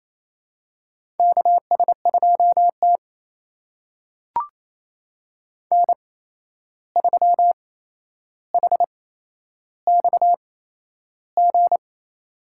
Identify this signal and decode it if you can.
Morse code, heard as a keyed tone, about 27 wpm, spaced as for 11 wpm, 700 Hz. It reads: KS2T N3HXG
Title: USA Call Signs - 8x Character Spacing - 28wpm